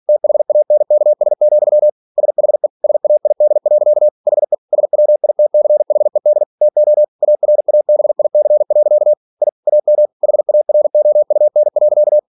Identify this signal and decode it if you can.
Morse code, heard as a keyed tone, about 46 wpm, 600 Hz. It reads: THANKS, SHE SAID. HE SWITCHED TO ARABIC. I AM HAROUN.
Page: Never 47wpm